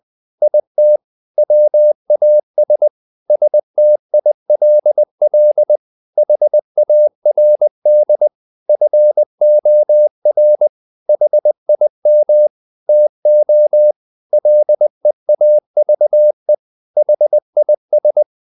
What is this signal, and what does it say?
Morse code, heard as a keyed tone, about 20 wpm, 600 Hz. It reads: IT WAS STILL HARD FOR HIM TO LEAVE HIS
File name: the old man and the sea / the old man and the sea part 1 20wpm